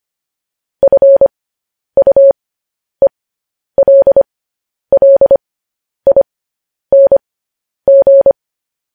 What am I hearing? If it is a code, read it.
Morse code, heard as a keyed tone, about 25 wpm, spaced as for 10 wpm, 550 Hz. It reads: FUELLING
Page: news headlines in morse code at 10 wpm